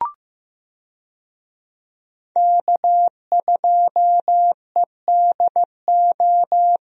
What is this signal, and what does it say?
Morse code, heard as a keyed tone, about 15 wpm, 700 Hz. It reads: K2EDO
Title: USA Call Signs - 5 Characters - Repeated 3x - 15wpm